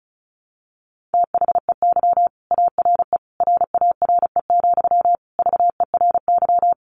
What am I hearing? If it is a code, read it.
Morse code, heard as a keyed tone, about 34 wpm, 700 Hz. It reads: THEY ARE RARE, VERY
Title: Sentences from Top 1100 Words 35wpm